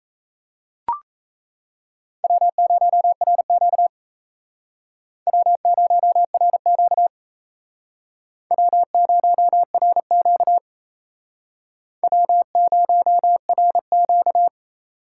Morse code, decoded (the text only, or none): W0RQ W0RQ W0RQ W0RQ